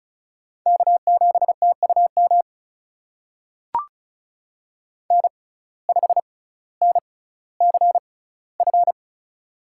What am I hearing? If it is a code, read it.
Morse code, heard as a keyed tone, about 36 wpm, spaced as for 18 wpm, 700 Hz. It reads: K7TUM N5NCF
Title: USA Call Signs - 5 Characters - 6x Character Spacing - 35wpm